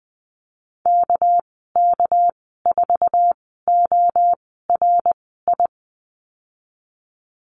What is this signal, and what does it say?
Morse code, heard as a keyed tone, about 19 wpm, spaced as for 14 wpm, 700 Hz. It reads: KK4ORI
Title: 200 Lesson 32g - Letter k - Call Signs 20wpm